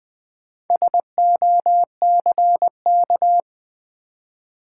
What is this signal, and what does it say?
Morse code, heard as a keyed tone, about 21 wpm, 700 Hz. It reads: SOCK
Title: Four Letter Words - 20wpm